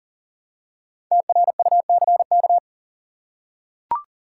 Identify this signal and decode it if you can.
Morse code, heard as a keyed tone, about 40 wpm, 700 Hz. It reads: TRUCK
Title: Five Letter Words - 40wpm